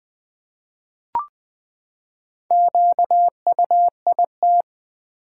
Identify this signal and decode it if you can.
Morse code, heard as a keyed tone, about 20 wpm, 700 Hz. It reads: QUIT
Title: Four Letter Words - 20wpm